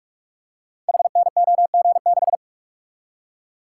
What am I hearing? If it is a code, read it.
Morse code, heard as a keyed tone, about 44 wpm, 700 Hz. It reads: HNOG6
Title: Code Groups - 5 Letter:Numbers - 45wpm